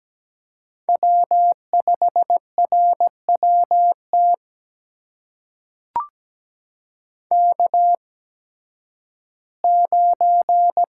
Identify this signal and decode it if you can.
Morse code, heard as a keyed tone, about 17 wpm, spaced as for 11 wpm, 700 Hz. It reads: W5RWT K9
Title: USA Call Signs - 5 Characters - 8x Character Spacing - 17wpm